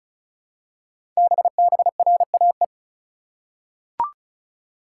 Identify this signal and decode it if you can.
Morse code, heard as a keyed tone, about 36 wpm, 690 Hz. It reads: BBRAE
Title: Code Groups - 5 Letters - 35wpm